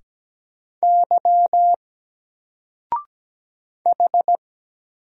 Morse code, heard as a keyed tone, about 17 wpm, 700 Hz. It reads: Y H
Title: Single Character (Letter:Number:Punctuation:Prosign) 17wpm